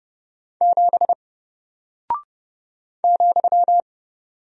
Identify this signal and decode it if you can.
Morse code, heard as a keyed tone, about 30 wpm, 700 Hz. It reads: 7 ,
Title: Single Character (Letter:Number:Punctuation:Prosign) Alt 30wpm